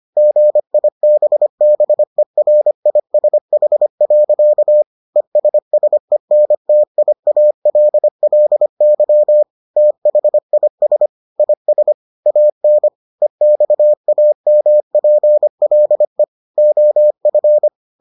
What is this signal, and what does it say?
Morse code, heard as a keyed tone, about 25 wpm, 600 Hz. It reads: GIBBERISH. ESSENTIALLY THIS IS AN EXAMPLE OF